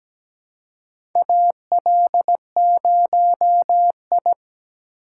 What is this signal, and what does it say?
Morse code, heard as a keyed tone, about 17 wpm, 700 Hz. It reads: AL0I
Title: USA Call Signs - 4 Characters - 3x Character Spacing - 17wpm